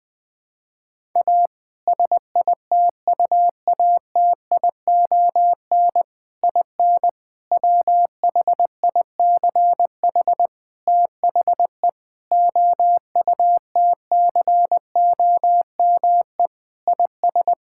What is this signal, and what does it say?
Morse code, heard as a keyed tone, about 21 wpm, 700 Hz. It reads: A SITUATION IN WHICH THE OUTCOME IS
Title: Sentences with 3-Word Phrases - 20wpm